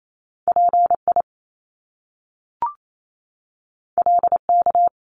Morse code, heard as a keyed tone, about 28 wpm, 700 Hz. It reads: PI LK
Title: Top Two Letter Combinations in Words 28wpm